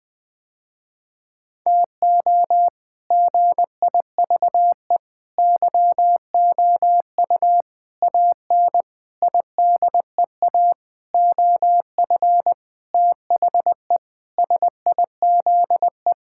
Morse code, heard as a keyed tone, about 20 wpm, 700 Hz. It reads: TO GIVE YOU AN IDEA OF THE SIZE
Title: Sentences with 6-Word Phrases - Spoken Beginning - 20wpm